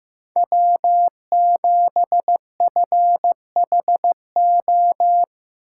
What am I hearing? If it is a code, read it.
Morse code, heard as a keyed tone, about 15 wpm, 700 Hz. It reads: W7FHO
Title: USA Call Signs - 5 Characters - 2x Character Spacing - 15wpm